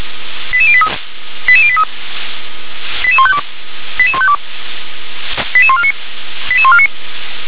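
Five-tone selective call (ZVEI), heard as quick runs of five tones, tones of about 70 ms. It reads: 88083 88083 88152 88152 88149 88149
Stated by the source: SDR receiver recording of ZVEI five-tone selective call 88148.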